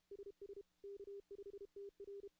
Morse code, heard as a keyed tone, about 32 wpm, 400 Hz. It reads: SS K5TR
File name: contests / CW-SS-k5tr- -0000